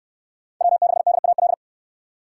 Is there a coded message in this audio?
Yes: Morse code from 0.6 to 1.5 s, heard as a keyed tone, about 69 wpm, 700 Hz. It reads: W6DAL